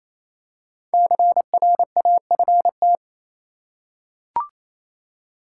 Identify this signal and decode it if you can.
Morse code, heard as a keyed tone, about 28 wpm, 700 Hz. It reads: CRAFT
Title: Five Letter Words - 28wpm